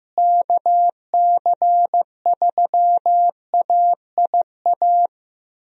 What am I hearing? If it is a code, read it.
Morse code, heard as a keyed tone, about 16 wpm, 700 Hz. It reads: KC3AIA
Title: USA Call Signs - 6 Characters - Repeated 3x - 15wpm